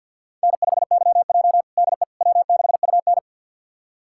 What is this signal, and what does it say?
Morse code, heard as a keyed tone, about 50 wpm, 700 Hz. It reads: N5YJ DE W6UN
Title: USA Callsign DE Callsign - 4 Characters - 50wpm